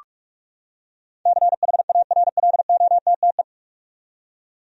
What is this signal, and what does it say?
Morse code, heard as a keyed tone, about 46 wpm, 700 Hz. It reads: CHARLOTTE
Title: USA Common Names - 45wpm